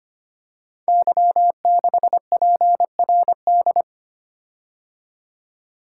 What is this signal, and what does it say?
Morse code, heard as a keyed tone, about 25 wpm, 700 Hz. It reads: Y6PRD
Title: Code Groups - 5 Letter:Numbers - 25wpm